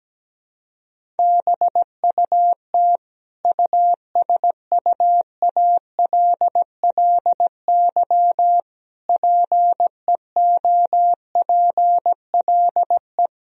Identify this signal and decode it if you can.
Morse code, heard as a keyed tone, about 17 wpm, 700 Hz. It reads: BUT USUALLY PEOPLE
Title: Sentences from Top 1000 Words - Review Part 3 - 17wpm